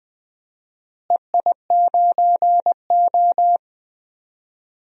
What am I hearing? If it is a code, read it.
Morse code, heard as a keyed tone, about 20 wpm, 700 Hz. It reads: EI9O
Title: DXCC-100 Call Signs - 4 Characters - Repeated 3x - 20wpm